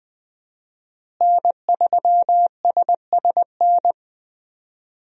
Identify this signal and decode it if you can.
Morse code, heard as a keyed tone, about 20 wpm, 700 Hz. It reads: N3SSN
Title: USA Call Signs - 5 Characters - Repeated 3x - 20wpm